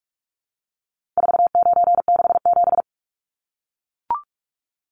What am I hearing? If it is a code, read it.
Morse code, heard as a keyed tone, about 44 wpm, 700 Hz. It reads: 4967
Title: Numbers - 4-Digit 45wpm